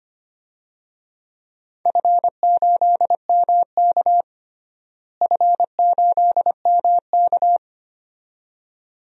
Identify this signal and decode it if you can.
Morse code, heard as a keyed tone, about 25 wpm, 700 Hz. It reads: F8MK F8MK
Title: DXCC-100 Call Signs - 4 Characters - Repeated 2x - 25wpm